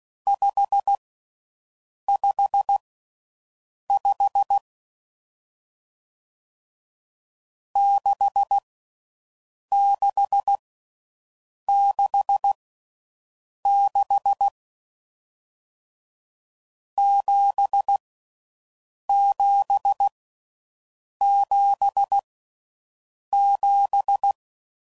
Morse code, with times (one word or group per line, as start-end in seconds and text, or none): 0.3-1.0 5
2.1-2.8 5
3.9-4.6 5
7.8-8.6 6
9.7-10.6 6
11.7-12.5 6
13.7-14.5 6
17.0-18.0 7
19.1-20.1 7
21.2-22.2 7
23.3-24.3 7